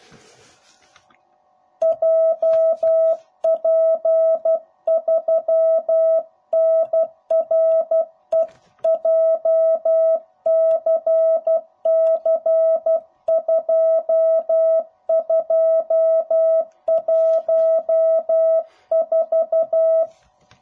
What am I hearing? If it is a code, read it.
Morse code, heard as a keyed tone, about 12 wpm, 660 Hz. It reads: JP3NRE JCC2214
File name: jh8cbh / 2021cw / cw05b